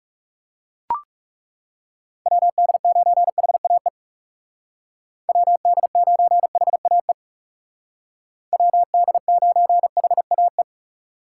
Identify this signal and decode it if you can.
Morse code, heard as a keyed tone, about 35 wpm, 700 Hz. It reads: WD9HAE WD9HAE WD9HAE
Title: Speed Racing - USA 2x3 Call Signs 30wpm